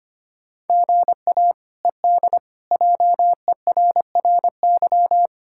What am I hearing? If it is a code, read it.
Morse code, heard as a keyed tone, about 25 wpm, 700 Hz. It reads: GA ED JERRY
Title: K1USN SST QSO Practice - 25wpm